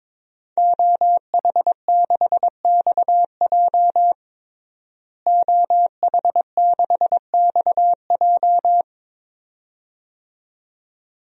Morse code, heard as a keyed tone, about 22 wpm, 700 Hz. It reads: OH6XJ OH6XJ
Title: DXCC-100 Call Signs - 5 Characters - Repeated 2x - 22wpm